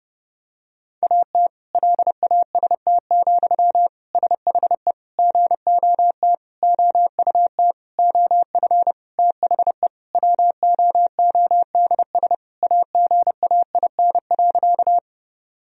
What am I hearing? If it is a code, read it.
Morse code, heard as a keyed tone, about 30 wpm, 700 Hz. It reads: AT LAST, SHE GOT OUT OF THE WOODS AGAIN.
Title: Sentences from Top 1000 Words - Review Part 2 - 30wpm